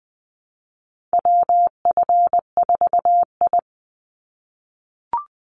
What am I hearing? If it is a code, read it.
Morse code, heard as a keyed tone, about 20 wpm, 700 Hz. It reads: WF4I
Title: NA SOTA Chaser Callsigns 20wpm